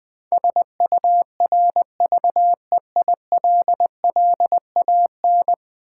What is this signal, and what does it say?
Morse code, heard as a keyed tone, about 20 wpm, 700 Hz. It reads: SURVEILLAN
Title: Build A Word - 5 Letters At A Time - 12-Letter Words - 20wpm